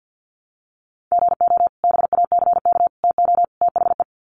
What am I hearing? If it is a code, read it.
Morse code, heard as a keyed tone, about 50 wpm, 700 Hz. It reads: GO BACK TO THE
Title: Sets of 4 Words 50wpm